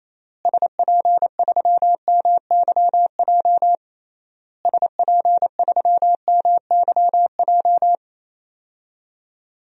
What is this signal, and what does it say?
Morse code, heard as a keyed tone, about 28 wpm, 700 Hz. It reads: SP3MYJ SP3MYJ
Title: DXCC-100 Call Signs - 6 Characters - Repeated 3x - 28wpm